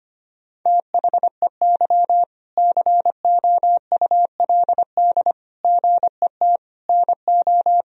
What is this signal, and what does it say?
Morse code, heard as a keyed tone, about 25 wpm, 700 Hz. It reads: THEY COULD GET NO
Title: Sentences from Top 1000 Words - Review Part 3 - 25wpm